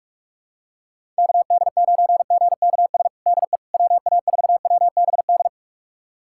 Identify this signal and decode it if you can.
Morse code, heard as a keyed tone, about 45 wpm, 700 Hz. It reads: KD9GKS DE WA4WBD
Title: USA Callsign DE Callsign - 6 Characters - 45wpm